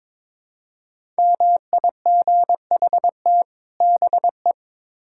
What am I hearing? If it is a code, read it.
Morse code, heard as a keyed tone, about 22 wpm, 700 Hz. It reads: MIGHT BE
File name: Build A Phrase - 2-Word Phrases - 22wpm